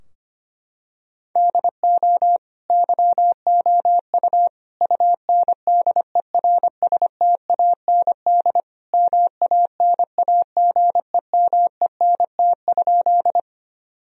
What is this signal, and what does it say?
Morse code, heard as a keyed tone, about 25 wpm, 700 Hz. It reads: DO YOU UNDERSTAND MANAGEMENT?
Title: Sentences from Top 1000 Words - Review Part 4 - 25wpm